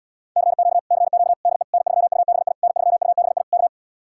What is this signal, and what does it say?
Morse code, heard as a keyed tone, about 77 wpm, 700 Hz. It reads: CQ CQ DE N3UBI N3UBI K